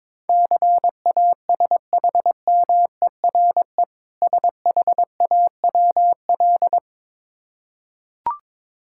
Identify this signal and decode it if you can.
Morse code, heard as a keyed tone, about 22 wpm, 690 Hz. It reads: CASHMERE SHAWL
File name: Difficult Words with Hints - 22wpm